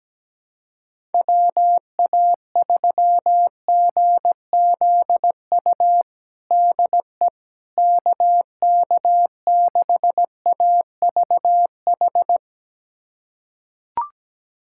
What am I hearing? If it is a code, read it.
Morse code, heard as a keyed tone, about 17 wpm, 700 Hz. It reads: WA3GZU DE KK6AVH
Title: USA Callsign DE Callsign - 6 Characters - 17wpm